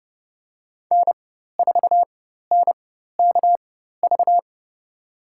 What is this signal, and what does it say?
Morse code, heard as a keyed tone, about 32 wpm, spaced as for 14 wpm, 700 Hz. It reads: N4NKV
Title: USA Call Signs - 5 Characters - 4x Character Spacing - 30wpm